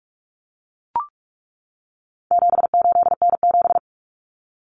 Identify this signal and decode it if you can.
Morse code, heard as a keyed tone, about 44 wpm, 700 Hz. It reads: 78N7
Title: Common Cut Numbers - 4-Digit 45wpm